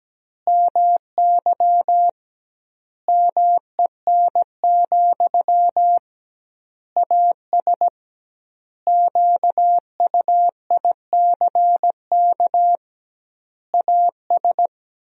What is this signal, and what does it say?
Morse code, heard as a keyed tone, about 17 wpm, 700 Hz. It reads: MY MEN, AS QUICK AS
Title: 2x Word-Spacing - Sentences from Top 500 Words - 17wpm